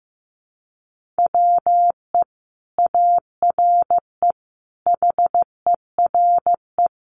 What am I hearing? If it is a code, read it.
Morse code, heard as a keyed tone, about 14 wpm, 700 Hz. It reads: WE ARE HERE
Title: Sentences from Top 1400 Words 15wpm